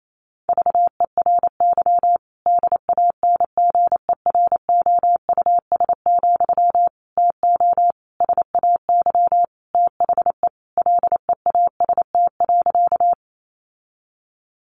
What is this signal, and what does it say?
Morse code, heard as a keyed tone, about 28 wpm, 700 Hz. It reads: VERY DANGEROUS, TO SAY THE LEAST.
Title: Sentences from Top 1200 Words 28wpm